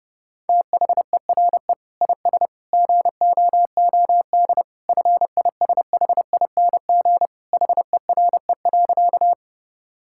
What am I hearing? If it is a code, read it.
Morse code, heard as a keyed tone, about 30 wpm, 700 Hz. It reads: THERE IS GOOD FISHING HERE.
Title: Sentences from Top 1000 Words - Review Part 5 - 30wpm